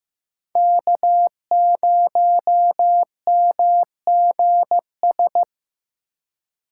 Morse code, heard as a keyed tone, about 17 wpm, 700 Hz. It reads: K0MGS
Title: Speed Racing - USA 1x3 Call Signs 15wpm